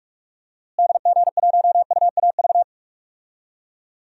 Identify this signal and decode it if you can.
Morse code, heard as a keyed tone, about 46 wpm, 700 Hz. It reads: DG1UAV
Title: DXCC-100 Call Signs - 6 Characters - Repeated 2x - 45wpm